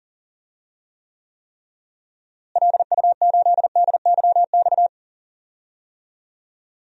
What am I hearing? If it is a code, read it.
Morse code, heard as a keyed tone, about 40 wpm, 700 Hz. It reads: LU8DYX
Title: DXCC-100 Call Signs - 6 Characters - Repeated 3x - 40wpm